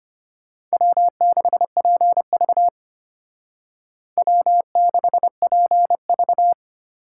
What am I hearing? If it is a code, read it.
Morse code, heard as a keyed tone, about 32 wpm, 700 Hz. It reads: W6PV W6PV